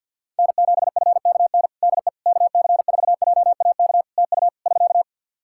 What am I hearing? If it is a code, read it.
Morse code, heard as a keyed tone, about 52 wpm, 700 Hz. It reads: N7FKN DE KC4JAK TU <SK>
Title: Practice QSO - Exchange 4 - Ending - 50wpm